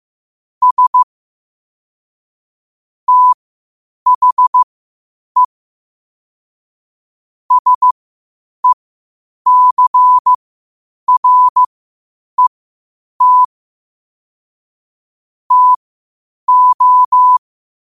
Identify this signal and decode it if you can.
Morse code, heard as a keyed tone, about 15 wpm, spaced as for 8 wpm, 1000 Hz. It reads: S THE SECRET TO
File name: qotdfiles / 8wpm